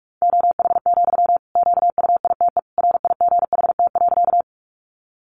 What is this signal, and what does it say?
Morse code, heard as a keyed tone, about 44 wpm, 700 Hz. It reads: OH, QUITE RIGHT.
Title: Sentences from Top 700 Words 45wpm